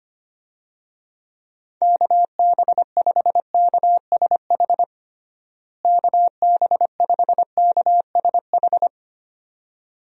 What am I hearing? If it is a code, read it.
Morse code, heard as a keyed tone, about 25 wpm, 700 Hz. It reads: KB5KSH KB5KSH
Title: USA Call Signs - 6 Characters - Repeated 2x - 25wpm